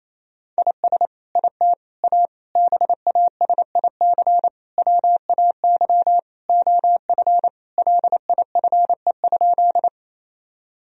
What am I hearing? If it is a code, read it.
Morse code, heard as a keyed tone, about 28 wpm, 700 Hz. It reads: IS IT A BASIC WAY OF LIFE?
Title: Sentences from Top 1000 Words - Review Part 3 - 28wpm